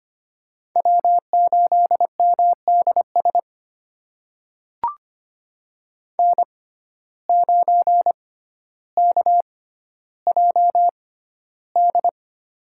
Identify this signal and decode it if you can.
Morse code, heard as a keyed tone, about 25 wpm, spaced as for 13 wpm, 700 Hz. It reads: W8MDS N9KJD
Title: USA Call Signs - 5 Characters - 6x Character Spacing - 25wpm